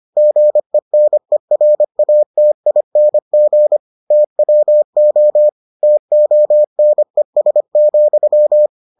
Morse code, heard as a keyed tone, about 25 wpm, 600 Hz. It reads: GENERATING TWO TONES,